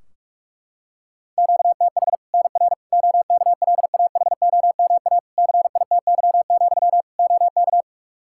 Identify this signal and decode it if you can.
Morse code, heard as a keyed tone, about 46 wpm, 700 Hz. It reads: QTH NR OKLAHOMA CITY, OK
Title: Practice QSO - Exchange 1 - Beginning - 45wpm